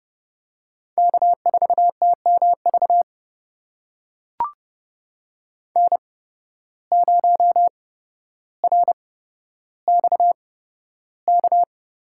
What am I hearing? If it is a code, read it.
Morse code, heard as a keyed tone, about 30 wpm, spaced as for 13 wpm, 700 Hz. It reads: K4TMV N0RXK
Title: USA Call Signs - 5 Characters - 8x Character Spacing - 30wpm